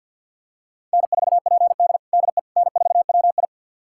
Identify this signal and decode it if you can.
Morse code, heard as a keyed tone, about 52 wpm, 700 Hz. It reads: N4PD DE N4WI
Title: USA Callsign DE Callsign - 4 Characters - 50wpm